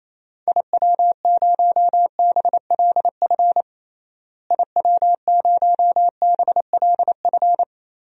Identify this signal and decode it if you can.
Morse code, heard as a keyed tone, about 27 wpm, 700 Hz. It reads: IW0BLF IW0BLF
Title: DXCC-100 Call Signs - 6 Characters - Repeated 3x - 28wpm